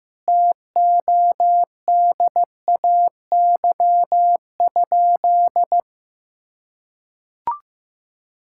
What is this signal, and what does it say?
Morse code, heard as a keyed tone, about 16 wpm, 700 Hz. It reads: TODAY?
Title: Sentences from Top 1600 Words 15wpm